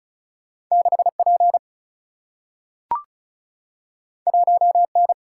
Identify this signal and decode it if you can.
Morse code, heard as a keyed tone, about 36 wpm, 700 Hz. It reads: 6P 1N
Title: Single-Letter Single-Number 35wpm